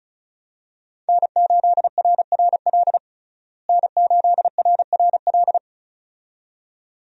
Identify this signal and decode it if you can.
Morse code, heard as a keyed tone, about 35 wpm, 700 Hz. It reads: N8RRL N8RRL
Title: USA Call Signs - 5 Characters - Repeated 2x - 35wpm